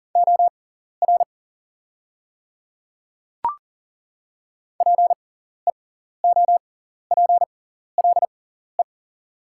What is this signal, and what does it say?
Morse code, heard as a keyed tone, about 40 wpm, spaced as for 14 wpm, 700 Hz. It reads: OR PEOPLE
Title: 6x Character Spacing - Top 100 Words - 40wpm